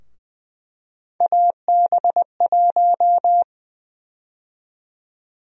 Morse code, heard as a keyed tone, about 20 wpm, 700 Hz. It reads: AB1
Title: USA Call Sign Prefixes - 20wpm